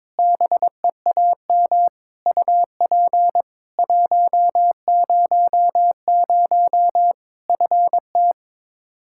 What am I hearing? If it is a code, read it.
Morse code, heard as a keyed tone, about 22 wpm, 700 Hz. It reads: BEAM UP 100 FT
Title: Practice QSO - Exchange 2 - Info - 22wpm